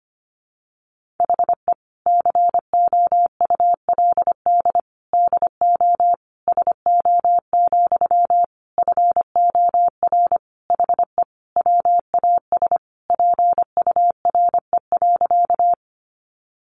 Morse code, heard as a keyed tone, about 25 wpm, 700 Hz. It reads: HE COULD DO SO, FOR HE WAS PURE.
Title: Sentences from Top 1500 Words 25wpm